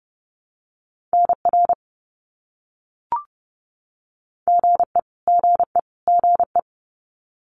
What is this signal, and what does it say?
Morse code, heard as a keyed tone, about 30 wpm, 700 Hz. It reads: NR GE GE GE
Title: Abbreviations Repeated 3x - 30wpm